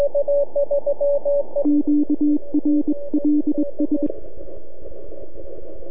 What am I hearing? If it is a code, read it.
Morse code, heard as a keyed tone, about 22 wpm, 300 Hz, with radio noise. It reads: QRLS
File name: NAQPCW-8-2009 / W1UJ NAQP CW 09 1800Z